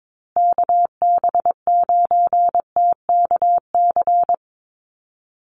Morse code, heard as a keyed tone, about 22 wpm, 700 Hz. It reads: KB9TKC
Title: USA Call Signs Repeated 2x 22wpm